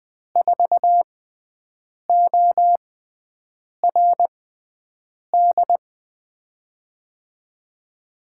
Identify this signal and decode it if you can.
Morse code, heard as a keyed tone, about 20 wpm, spaced as for 7 wpm, 700 Hz. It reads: 4ORD
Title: USA Call Signs - 6x Character Spacing - 20wpm